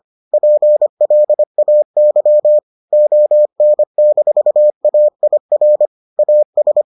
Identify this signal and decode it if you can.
Morse code, heard as a keyed tone, about 25 wpm, 600 Hz. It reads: PLAY ON-AIR AS